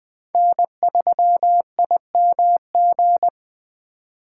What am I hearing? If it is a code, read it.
Morse code, heard as a keyed tone, about 20 wpm, 700 Hz. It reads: N3IMG